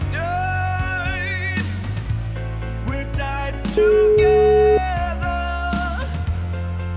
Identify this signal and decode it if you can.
Ringback tone (Europe, 425 Hz), with noise.